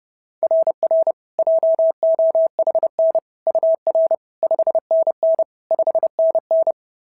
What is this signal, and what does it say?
Morse code, heard as a keyed tone, about 30 wpm, 690 Hz. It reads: RR JOHN UR 5NN 5NN
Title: SOTA QSO Practice 30wpm